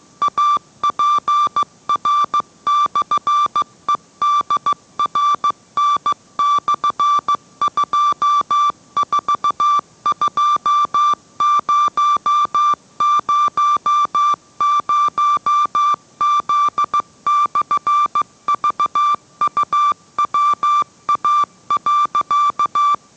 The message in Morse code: APR/EDRN/242000Z/VUWA.

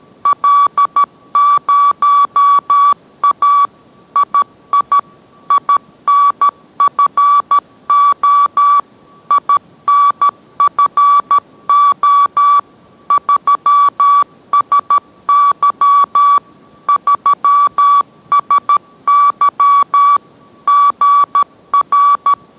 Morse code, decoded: L0A II INFO INFO 3SY 3SY GR

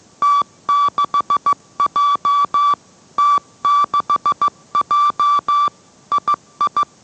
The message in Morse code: T6J T6J II

